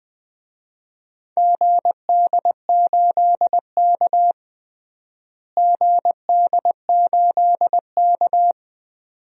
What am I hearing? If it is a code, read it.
Morse code, heard as a keyed tone, about 21 wpm, 700 Hz. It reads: GD8K GD8K
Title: DXCC-100 Call Signs - 4 Characters - Repeated 2x - 20wpm